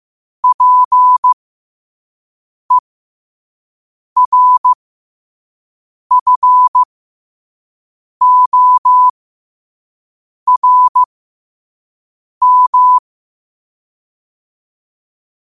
Morse code, heard as a keyed tone, about 15 wpm, spaced as for 5 wpm, 1000 Hz. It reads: PERFORM
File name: qotdfiles / 5wpm